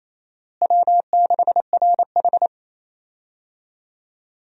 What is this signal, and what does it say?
Morse code, heard as a keyed tone, about 28 wpm, 700 Hz. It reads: W6RH
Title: USA Call Signs - 4 Characters - Repeated 3x - 28wpm